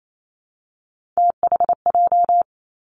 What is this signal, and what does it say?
Morse code, heard as a keyed tone, about 28 wpm, 700 Hz. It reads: THJ
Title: Three Challenging Letter-Number Combinations 28wpm